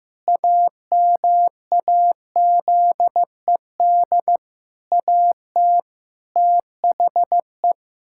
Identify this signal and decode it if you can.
Morse code, heard as a keyed tone, about 15 wpm, 700 Hz. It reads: AMAZED AT THE
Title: Pangram Challenge v2 - 15wpm